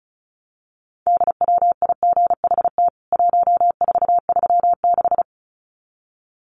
Morse code, heard as a keyed tone, about 34 wpm, 700 Hz. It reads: DWIGHT 1436
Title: CWT Exchange 35wpm